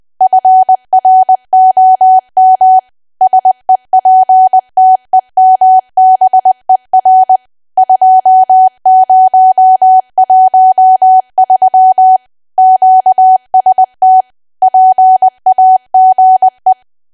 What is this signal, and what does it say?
Morse code, heard as a keyed tone, about 20 wpm, 750 Hz. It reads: FROM SEPTEMBER 2013 QST PAGE
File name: cwa / W1AW-Level3 / 20WPM